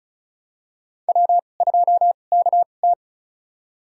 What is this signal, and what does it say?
Morse code, heard as a keyed tone, about 36 wpm, spaced as for 25 wpm, 700 Hz. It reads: W2KT
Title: USA Call Signs - 4 Characters - 2x Character Spacing - 35wpm